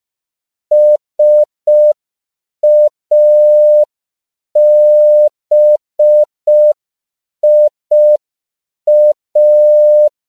Morse code, heard as a keyed tone, about 5 wpm, 600 Hz. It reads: SABIA